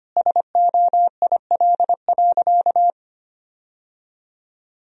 Morse code, heard as a keyed tone, about 25 wpm, 700 Hz. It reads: SOIL.
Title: Sentences from Top 1000 Words - Review Part 4 - 25wpm